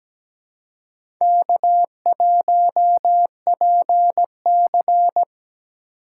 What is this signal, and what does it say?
Morse code, heard as a keyed tone, about 17 wpm, 700 Hz. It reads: K1PC
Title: USA Call Signs - 4 Characters - 4x Character Spacing - 17wpm